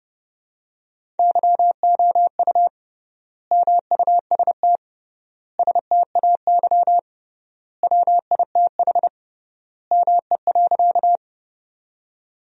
Morse code, heard as a keyed tone, about 28 wpm, 700 Hz. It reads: YOU MUST STAY WITH ME.
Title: 3x Word-Spacing - Sentences from Top 500 Words - 30wpm